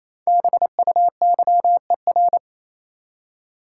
Morse code, heard as a keyed tone, about 28 wpm, 700 Hz. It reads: BUYER
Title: Five Letter Words - 28wpm